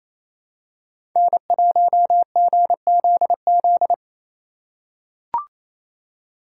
Morse code, heard as a keyed tone, about 28 wpm, 700 Hz. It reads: N1GZZ
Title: USA Call Signs - 5 Characters - 2x Character Spacing - 28wpm